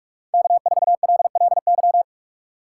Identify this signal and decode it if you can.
Morse code, heard as a keyed tone, about 44 wpm, 700 Hz. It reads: K4LLY
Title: USA Call Signs - 6x Character Spacing - 45wpm